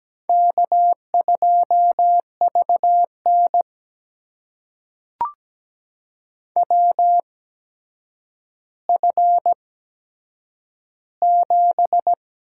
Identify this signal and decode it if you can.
Morse code, heard as a keyed tone, about 17 wpm, spaced as for 8 wpm, 700 Hz. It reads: K2VN WF7
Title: USA Call Signs - 4 Characters - 8x Character Spacing - 17wpm